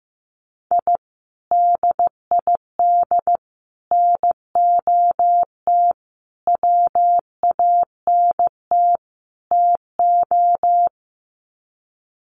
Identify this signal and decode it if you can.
Morse code, heard as a keyed tone, about 15 wpm, 700 Hz. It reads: I DID NOT WANT TO
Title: Sets of 5 Words 15wpm